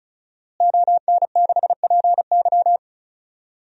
Morse code, heard as a keyed tone, about 34 wpm, 700 Hz. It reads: ON6PY
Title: DXCC-100 Call Signs - 5 Characters - Repeated 2x - 35wpm